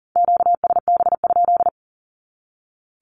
Morse code, heard as a keyed tone, about 40 wpm, 700 Hz. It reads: QSB?
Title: Q-Codes 40wpm